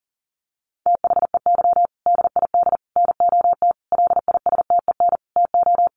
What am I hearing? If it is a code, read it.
Morse code, heard as a keyed tone, about 40 wpm, 700 Hz. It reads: THEY DID NOT LISTEN TO
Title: Sentences from Top 600 Words 40wpm